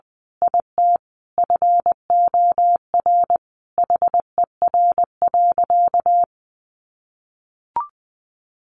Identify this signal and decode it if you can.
Morse code, heard as a keyed tone, about 20 wpm, 700 Hz. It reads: IT FOR HER.
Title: Sentences from Top 1400 Words 20wpm